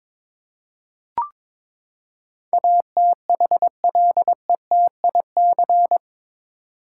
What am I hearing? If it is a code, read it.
Morse code, heard as a keyed tone, about 22 wpm, 700 Hz. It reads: ATHLETIC
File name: practice / 22wpm / Eight Letter Words - 22wpm